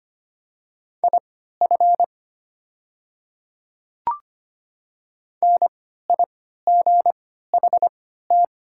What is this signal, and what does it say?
Morse code, heard as a keyed tone, about 25 wpm, spaced as for 14 wpm, 690 Hz. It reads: IF NIGHT
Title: 3x Character Spacing - Top 300 Words - 25wpm